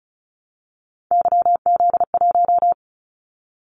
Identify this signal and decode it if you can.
Morse code, heard as a keyed tone, about 36 wpm, 690 Hz. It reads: YZ1
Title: Callsign Prefixes for Top 25 DXCC - 35wpm